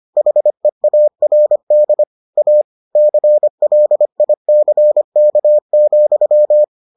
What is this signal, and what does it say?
Morse code, heard as a keyed tone, about 25 wpm, 600 Hz. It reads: HEARD A CLICK,